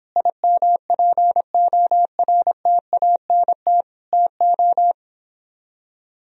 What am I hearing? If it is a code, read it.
Morse code, heard as a keyed tone, about 28 wpm, 700 Hz. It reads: IMPORTANT TO